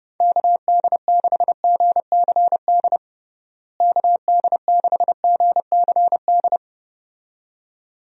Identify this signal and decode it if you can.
Morse code, heard as a keyed tone, about 30 wpm, 700 Hz. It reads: KD6GCD KD6GCD
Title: USA Call Signs - 6 Characters - Repeated 2x - 30wpm